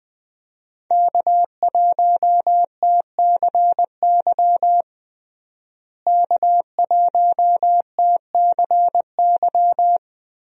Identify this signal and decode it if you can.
Morse code, heard as a keyed tone, about 21 wpm, 700 Hz. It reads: K1TCY K1TCY
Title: USA Call Signs - 5 Characters - Repeated 3x - 20wpm